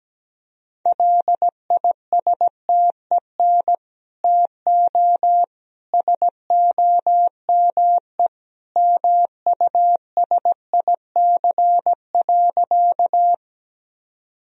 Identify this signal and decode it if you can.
Morse code, heard as a keyed tone, about 17 wpm, 700 Hz. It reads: LISTEN TO SOME MUSIC.
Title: Sentences from Top 1000 Words - Review Part 1 - 17wpm